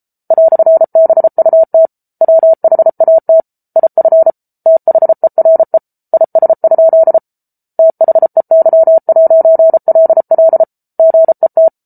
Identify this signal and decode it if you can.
Morse code, heard as a keyed tone, about 34 wpm, 650 Hz. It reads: "BUT WHAT IF THERE IS? THEY'LL GET